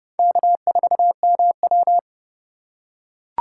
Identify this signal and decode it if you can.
Morse code, heard as a keyed tone, about 30 wpm, 700 Hz. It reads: K4MW
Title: USA Call Signs - 2x Character Spacing - 30wpm